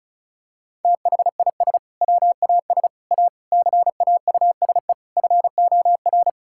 Morse code, heard as a keyed tone, about 36 wpm, 690 Hz. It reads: THIS WAS A CAUSE FOR
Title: Sentences from Top 1600 Words 35wpm